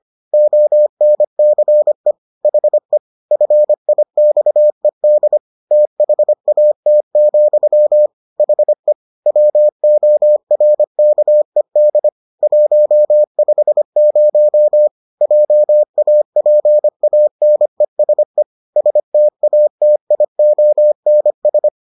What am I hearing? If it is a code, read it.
Morse code, heard as a keyed tone, about 25 wpm, 600 Hz. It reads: ONCE HE FIXED THAT, HE WORKED 150 JAPANESE STATIONS